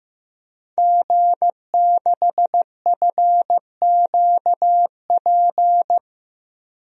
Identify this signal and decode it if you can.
Morse code, heard as a keyed tone, about 15 wpm, 700 Hz. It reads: G6FQP
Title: DXCC-100 Call Signs - 5 Characters - Repeated 3x - 15wpm